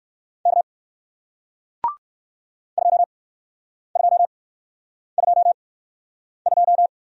Morse code, heard as a keyed tone, about 61 wpm, spaced as for 9 wpm, 700 Hz. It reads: K 2222